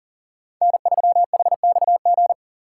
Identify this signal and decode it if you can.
Morse code, heard as a keyed tone, about 40 wpm, 700 Hz. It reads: N3HXG
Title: USA Call Signs - 5 Characters - 2x Character Spacing - 40wpm